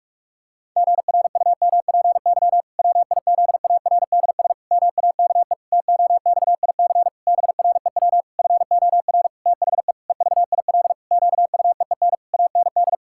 Morse code, heard as a keyed tone, about 46 wpm, 700 Hz. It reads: GRUMPY WIZARDS MAKE TOXIC BREW FOR THE EVIL QUEEN AND
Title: Pangram Challenge v2 - 45wpm